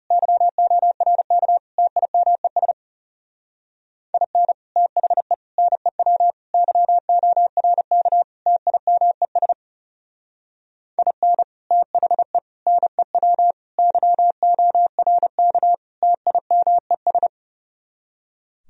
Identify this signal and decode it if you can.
Morse code, heard as a keyed tone, about 37 wpm, 700 Hz. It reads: YORK TIMES IN THE NEW YORK TIMES IN THE NEW YORK TIMES
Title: Speed-Racing - Sets of 5 Words 30wpm